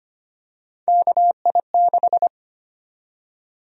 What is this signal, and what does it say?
Morse code, heard as a keyed tone, about 25 wpm, 700 Hz. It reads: KI6
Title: USA Call Sign Prefixes - Repeated 2x - 25wpm